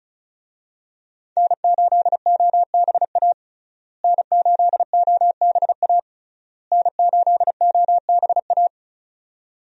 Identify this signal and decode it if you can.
Morse code, heard as a keyed tone, about 36 wpm, 700 Hz. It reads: N8OBA N8OBA N8OBA
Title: USA Call Signs - 5 Characters - Repeated 3x - 35wpm